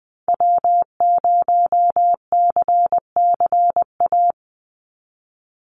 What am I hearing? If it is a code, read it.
Morse code, heard as a keyed tone, about 20 wpm, 700 Hz. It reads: W0CCA
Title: NA SOTA Activator Callsigns 20wpm